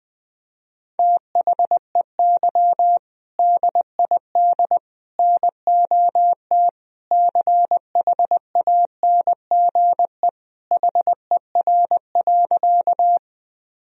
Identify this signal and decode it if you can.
Morse code, heard as a keyed tone, about 20 wpm, 700 Hz. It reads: THEY DID NOT CHANGE HER.
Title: Sentences from Top 1000 Words - Review Part 1 - 20wpm